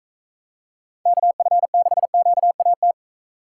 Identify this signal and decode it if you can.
Morse code, heard as a keyed tone, about 45 wpm, 700 Hz. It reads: KF6QAT